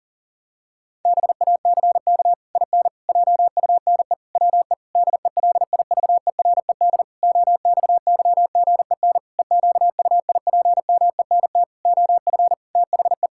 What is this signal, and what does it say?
Morse code, heard as a keyed tone, about 40 wpm, 700 Hz. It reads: BACK IN JUNE WE DELIVERED OXYGEN EQUIPMENT OF THE
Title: Pangram Challenge v2 - 40wpm